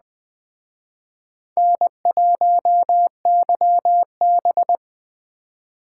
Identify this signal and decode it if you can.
Morse code, heard as a keyed tone, about 21 wpm, 700 Hz. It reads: N1YB
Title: Speed Racing - USA Call Signs 17wpm